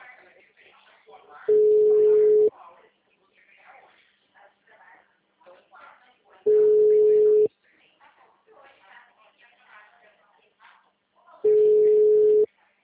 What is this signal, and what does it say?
Ringback tone (Europe, 425 Hz), with speech.